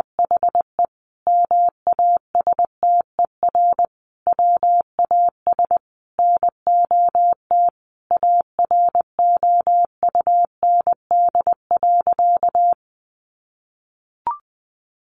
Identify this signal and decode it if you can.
Morse code, heard as a keyed tone, about 20 wpm, 700 Hz. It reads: HE MASTER WAS NOT AROUND.
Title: Sentences from Top 1300 Words 20wpm